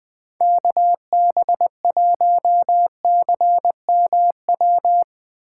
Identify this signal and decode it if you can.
Morse code, heard as a keyed tone, about 20 wpm, 700 Hz. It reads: KB1CMW